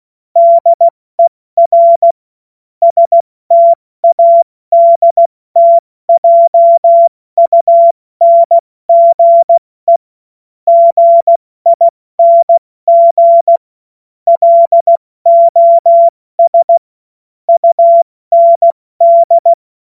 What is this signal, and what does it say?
Morse code, heard as a keyed tone, about 16 wpm, 680 Hz. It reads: DER STADTJUNGE GING LOS UND
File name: Morsekurs / Witze / Witz 07. März 2023 70 F80